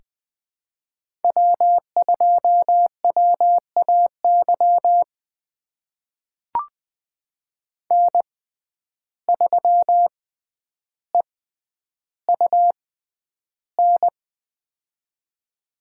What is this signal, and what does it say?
Morse code, heard as a keyed tone, about 20 wpm, spaced as for 10 wpm, 700 Hz. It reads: W2WAY N3EUN
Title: USA Call Signs - 5 Characters - 6x Character Spacing - 20wpm